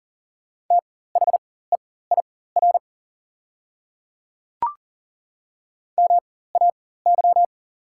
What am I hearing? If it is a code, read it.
Morse code, heard as a keyed tone, about 40 wpm, spaced as for 19 wpm, 690 Hz. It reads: THEIR MAY
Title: 4x Character Spacing - Top 100 Words - 40wpm